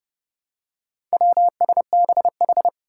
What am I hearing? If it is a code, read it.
Morse code, heard as a keyed tone, about 30 wpm, 700 Hz. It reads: WSBH
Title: Four Challenging Letter-Number Combinations 30wpm